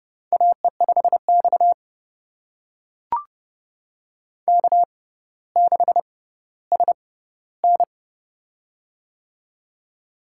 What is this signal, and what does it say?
Morse code, heard as a keyed tone, about 30 wpm, spaced as for 15 wpm, 700 Hz. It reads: AE5X K6SN
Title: USA Call Signs - 4 Characters - 6x Character Spacing - 30wpm